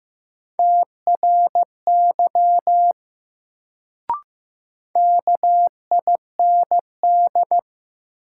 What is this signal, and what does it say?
Morse code, heard as a keyed tone, about 15 wpm, 700 Hz. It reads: TRY KIND
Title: Sets of 2 Words - New Word Drill - 15wpm